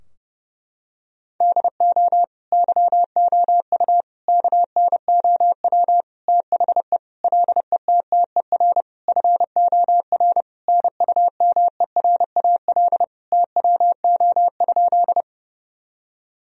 Morse code, heard as a keyed tone, about 30 wpm, 700 Hz. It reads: DO YOU KNOW THE LETTER FOR NUMERAL TWO?
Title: Sentences from Top 1000 Words - Review Part 2 - 30wpm